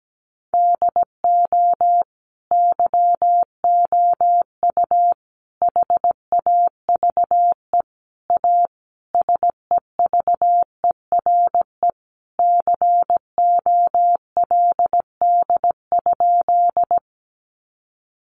Morse code, heard as a keyed tone, about 17 wpm, 700 Hz. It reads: DO YOU HAVE A SEVERE COLD?
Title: Sentences from Top 1400 Words 17wpm